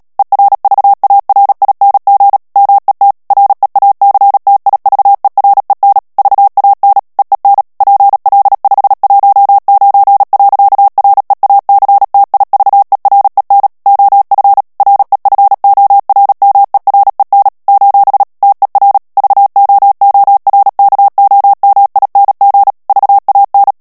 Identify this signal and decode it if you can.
Morse code, heard as a keyed tone, about 37 wpm, 800 Hz. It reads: ERVARING MET REACTIVEREN VAN EEN PL519. REACTIVEREN OF REFORMEREN : TER VOORKOMING VAN
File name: morse mp3 / HOLLANDAIS texte clair 2 vitesse 35 mots minute